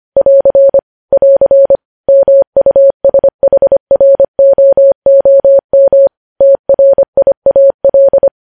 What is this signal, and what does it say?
Morse code, heard as a keyed tone, about 25 wpm, 550 Hz. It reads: <AR> <AR> MUSHROOM TRIAL
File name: podcast / News headlines-1x-25x25